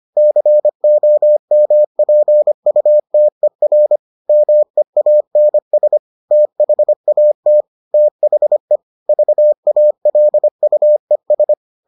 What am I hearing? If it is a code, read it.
Morse code, heard as a keyed tone, about 25 wpm, 600 Hz. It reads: COMPUTER MEANS THAT THE VALUES